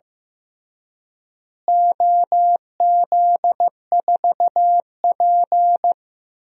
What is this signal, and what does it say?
Morse code, heard as a keyed tone, about 15 wpm, 700 Hz. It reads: OZ4P
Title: DXCC-100 Call Signs - 4 Characters - Repeated 2x - 15wpm